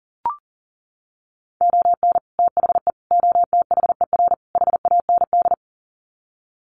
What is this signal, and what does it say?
Morse code, heard as a keyed tone, about 40 wpm, 700 Hz. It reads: ON THE OTHER HAND
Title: Sets of 4 Words 40wpm